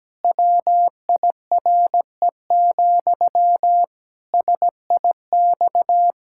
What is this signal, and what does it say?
Morse code, heard as a keyed tone, about 17 wpm, 700 Hz. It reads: WIRE, SIX
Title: Pangram Challenge v2 - 17wpm